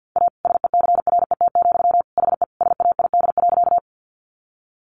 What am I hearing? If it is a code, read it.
Morse code, heard as a keyed tone, about 50 wpm, 700 Hz. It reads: A SECRET, HE SAID.
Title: Sentences from Top 1200 Words 50wpm